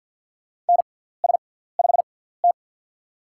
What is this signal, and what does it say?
Morse code, heard as a keyed tone, about 47 wpm, spaced as for 17 wpm, 700 Hz. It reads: NS5T
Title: USA Call Signs - 4 Characters - 6x Character Spacing - 50wpm